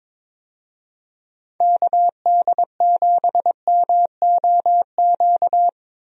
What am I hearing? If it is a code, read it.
Morse code, heard as a keyed tone, about 23 wpm, 700 Hz. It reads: KD7MOQ
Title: USA Call Signs - 6 Characters - Repeated 3x - 22wpm